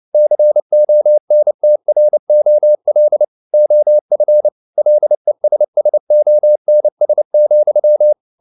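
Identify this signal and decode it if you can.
Morse code, heard as a keyed tone, about 29 wpm, 600 Hz. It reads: CONTROL OF LESSONS,
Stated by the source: Morse code practice at 28 wpm, text of Ken Follett's novel Never in English, from CW med Gnister.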